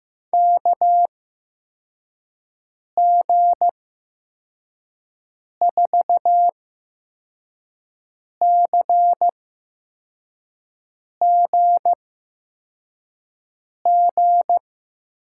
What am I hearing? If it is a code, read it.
Morse code, heard as a keyed tone, about 15 wpm, spaced as for 4 wpm, 700 Hz. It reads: KG4CGG
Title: USA Call Signs - 8x Character Spacing - 15wpm